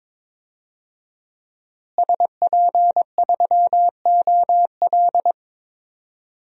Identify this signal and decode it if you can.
Morse code, heard as a keyed tone, about 22 wpm, 700 Hz. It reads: SP3OL
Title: DXCC-100 Call Signs - 5 Characters - Repeated 2x - 22wpm